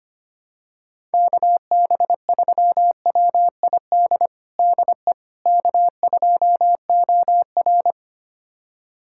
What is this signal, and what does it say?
Morse code, heard as a keyed tone, about 25 wpm, 700 Hz. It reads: KB3WID DE K2OR
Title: USA Callsign DE Callsign 25wpm